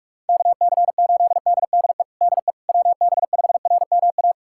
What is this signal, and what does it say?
Morse code, heard as a keyed tone, about 46 wpm, 700 Hz. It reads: KC8DDE DE WB5RMA
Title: USA Callsign DE Callsign - 6 Characters - 45wpm